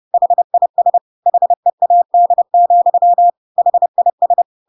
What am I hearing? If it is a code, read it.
Morse code, heard as a keyed tone, about 30 wpm, 700 Hz. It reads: HIS HEAD, HIS
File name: Pinocchio / Pinocchio ch1 30 30 ik1hge